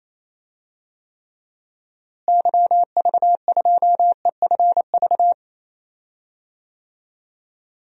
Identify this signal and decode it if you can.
Morse code, heard as a keyed tone, about 27 wpm, 700 Hz. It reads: YV2EFV
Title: DXCC-100 Call Signs - 6 Characters - Repeated 2x - 28wpm